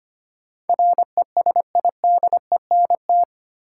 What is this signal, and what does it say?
Morse code, heard as a keyed tone, about 25 wpm, 700 Hz. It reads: RESIDENT
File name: Eight Letter Words - 25wpm